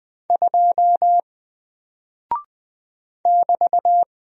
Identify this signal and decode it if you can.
Morse code, heard as a keyed tone, about 20 wpm, 700 Hz. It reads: 2 <BT>
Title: Single Character (Letter:Number:Punctuation:Prosign) Alt 20wpm